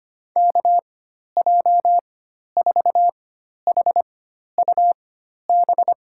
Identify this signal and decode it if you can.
Morse code, heard as a keyed tone, about 25 wpm, spaced as for 12 wpm, 700 Hz. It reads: KJ4HUB
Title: USA Call Signs - 4x Character Spacing - 25wpm